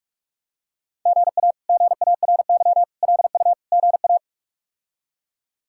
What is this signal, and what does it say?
Morse code, heard as a keyed tone, about 41 wpm, 700 Hz. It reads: GA GARY LU GA
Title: K1USN SST QSO Practice - 45wpm